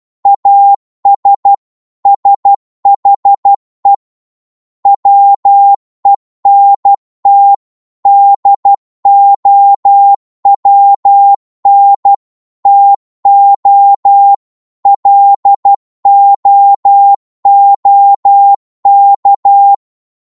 Morse code, heard as a keyed tone, about 12 wpm, 800 Hz. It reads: AS SHE WENT DOWN TO LOOK
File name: cwmp3 / psg-1-12wpm